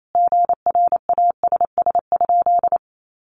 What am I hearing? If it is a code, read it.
Morse code, heard as a keyed tone, about 28 wpm, 700 Hz. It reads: GRASS?
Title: A Sound Of Thunder 28wpm